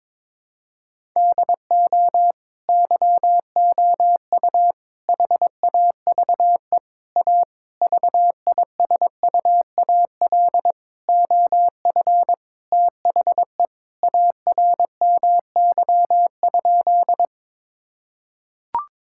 Morse code, heard as a keyed tone, about 22 wpm, 700 Hz. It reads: DO YOU HAVE A VISUAL OF THE ARMY?
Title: Sentences from Top 1000 Words - Review Part 6 - 22wpm